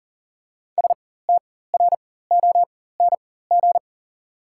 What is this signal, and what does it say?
Morse code, heard as a keyed tone, about 40 wpm, spaced as for 19 wpm, 690 Hz. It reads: STRONG
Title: 4x Character Spacing - Top 500 Words - 40wpm